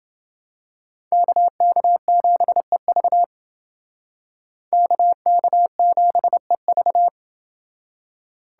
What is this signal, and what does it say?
Morse code, heard as a keyed tone, about 28 wpm, 700 Hz. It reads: KK7EV KK7EV